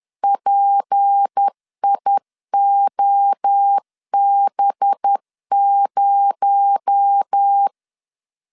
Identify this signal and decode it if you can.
Morse code, heard as a keyed tone, about 10 wpm, 800 Hz. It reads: PIOB0